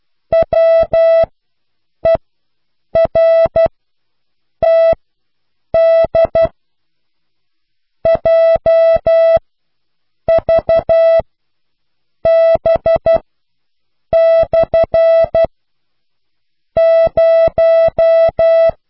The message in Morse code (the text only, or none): WERTD JVB/ 0